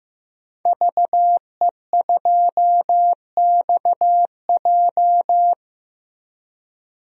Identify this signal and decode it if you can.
Morse code, heard as a keyed tone, about 15 wpm, 700 Hz. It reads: VE2XJ
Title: DXCC-100 Call Signs - 5 Characters - Repeated 3x - 15wpm